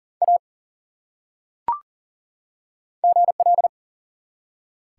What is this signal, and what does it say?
Morse code, heard as a keyed tone, about 38 wpm, 700 Hz. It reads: A GL
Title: Top 180 QSO Elements 40wpm